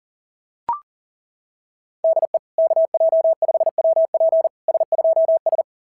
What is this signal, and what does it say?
Morse code, heard as a keyed tone, about 40 wpm, 690 Hz. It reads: DE KJ5WP S2S
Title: SOTA QSO Practice 40wpm